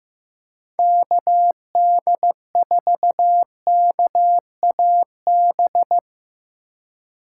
Morse code, heard as a keyed tone, about 15 wpm, 700 Hz. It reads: KD4KAB
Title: USA Call Signs - 4x Character Spacing - 15wpm